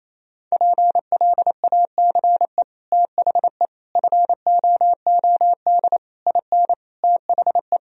Morse code, heard as a keyed tone, about 28 wpm, 700 Hz. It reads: PLACE THE FOOD IN THE
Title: Sentences from Top 1000 Words - Review Part 4 - 28wpm